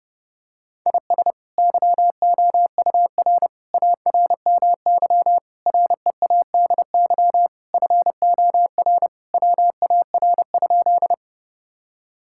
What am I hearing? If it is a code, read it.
Morse code, heard as a keyed tone, about 30 wpm, 700 Hz. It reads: IS YOUR ARMY READY FOR WAR?
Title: Sentences from Top 1000 Words - Review Part 5 - 30wpm